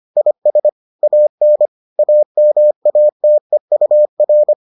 Morse code, heard as a keyed tone, about 25 wpm, 600 Hz. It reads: IS AN AMATEUR